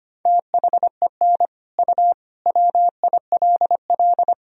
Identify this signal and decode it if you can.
Morse code, heard as a keyed tone, about 25 wpm, 700 Hz. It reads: THEN U WILL
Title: Sentences from Top 400 Words and Abbreviations 25wpm